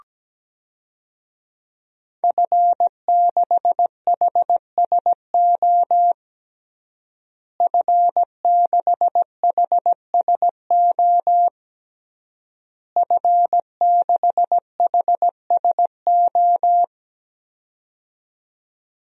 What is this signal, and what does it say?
Morse code, heard as a keyed tone, about 17 wpm, 700 Hz. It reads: F6HSO F6HSO F6HSO
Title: DXCC-100 Call Signs - 5 Characters - Repeated 3x - 17wpm